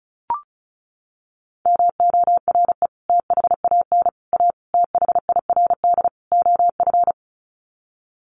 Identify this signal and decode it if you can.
Morse code, heard as a keyed tone, about 34 wpm, 700 Hz. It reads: MORE THAN A THIRD OF
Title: Sets of 5 Words Encore2 35wpm